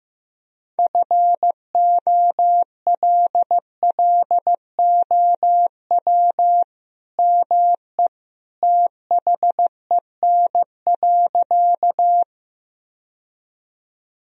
Morse code, heard as a keyed tone, about 15 wpm, 700 Hz. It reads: FOLLOW ME THEN.
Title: Sentences from Top 1000 Words - Review Part 1 - 15wpm